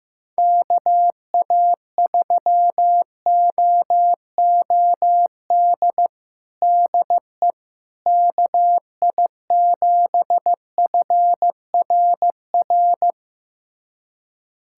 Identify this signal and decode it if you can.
Morse code, heard as a keyed tone, about 15 wpm, 700 Hz. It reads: KA3OOD DE KI7FRR
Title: USA Callsign DE Callsign 15wpm